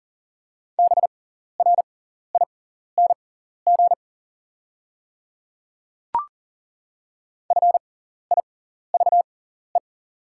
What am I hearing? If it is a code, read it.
Morse code, heard as a keyed tone, about 40 wpm, spaced as for 14 wpm, 700 Hz. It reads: BRING FIVE
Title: 6x Character Spacing - Top 500 Words - 40wpm